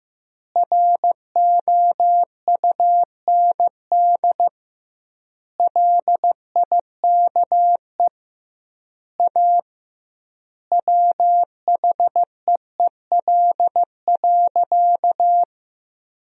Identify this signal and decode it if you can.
Morse code, heard as a keyed tone, about 15 wpm, 700 Hz. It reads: ROUND LIKE A WHEEL.
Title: 2x Word-Spacing - Sentences from Top 500 Words - 15wpm